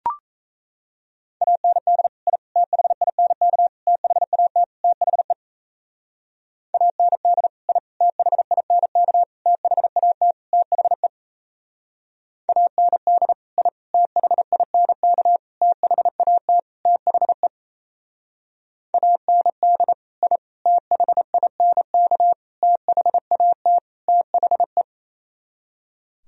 Morse code, heard as a keyed tone, about 33 wpm, 700 Hz. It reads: AND I THINK THAT THE AND I THINK THAT THE AND I THINK THAT THE AND I THINK THAT THE
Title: Speed-Racing - Sets of 5 Words 28wpm